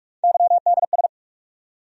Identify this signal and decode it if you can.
Morse code, heard as a keyed tone, about 46 wpm, 700 Hz. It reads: YDS
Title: USA Call Sign Postfixes - 3 Letters 45wpm